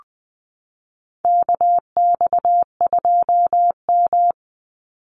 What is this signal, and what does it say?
Morse code, heard as a keyed tone, about 20 wpm, 700 Hz. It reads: KX2M
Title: USA Call Signs 20wpm